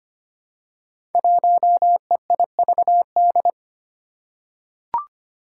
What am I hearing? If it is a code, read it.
Morse code, heard as a keyed tone, about 25 wpm, 700 Hz. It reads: 1EIVD
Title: Five Challenging Letter-Number Combinations 25wpm